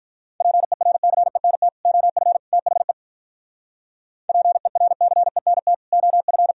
Morse code, heard as a keyed tone, about 44 wpm, 700 Hz. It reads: PERCENT OF THE PERCENT OF